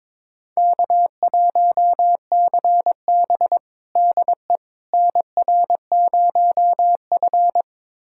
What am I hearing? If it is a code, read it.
Morse code, heard as a keyed tone, about 22 wpm, 700 Hz. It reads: K1CB DE NR0F
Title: USA Callsign DE Callsign - 4 Characters - 22wpm